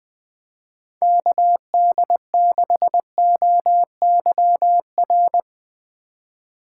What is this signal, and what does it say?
Morse code, heard as a keyed tone, about 20 wpm, 700 Hz. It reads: KD6OYR
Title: Speed Racing - USA 2x3 Call Signs 20wpm